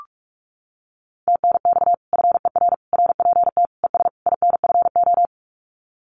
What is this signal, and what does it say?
Morse code, heard as a keyed tone, about 46 wpm, 700 Hz. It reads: TNX FER RPT ES INFO
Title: Common QSO Phrases Encore 45wpm